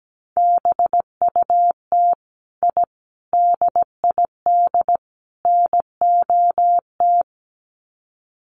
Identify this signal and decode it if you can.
Morse code, heard as a keyed tone, about 17 wpm, 700 Hz. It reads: BUT I DID NOT
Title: Sets of 4 Words 17wpm